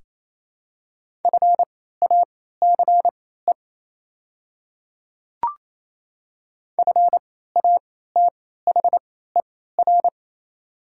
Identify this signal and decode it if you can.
Morse code, heard as a keyed tone, about 28 wpm, spaced as for 16 wpm, 700 Hz. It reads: FACE FATHER
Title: 3x Character Spacing - Top 300 Words - 28wpm